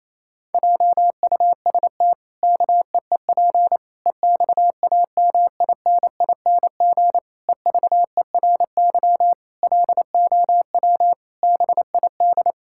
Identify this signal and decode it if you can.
Morse code, heard as a keyed tone, about 28 wpm, 700 Hz. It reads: JUST KEEP EXAMINING EVERY LOW BID
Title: Pangram Challenge v2 - 28wpm